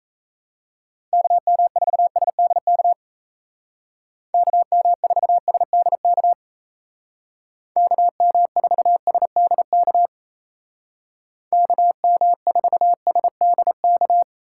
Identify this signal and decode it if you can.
Morse code, heard as a keyed tone, about 37 wpm, 700 Hz. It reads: KM4SDK KM4SDK KM4SDK KM4SDK